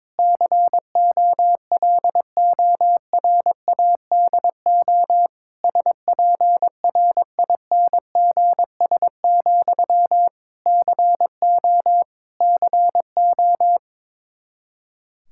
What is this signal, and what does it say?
Morse code, heard as a keyed tone, about 22 wpm, 700 Hz. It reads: COLORADO SPRINGS, CO CO
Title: K1USN SST QSO Practice 22wpm